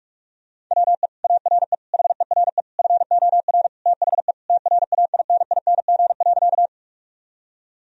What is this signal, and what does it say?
Morse code, heard as a keyed tone, about 48 wpm, 700 Hz. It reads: WE ARE HERE FOR THE TRAINING.
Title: Sentences from Top 1000 Words - Review Part 4 - 45wpm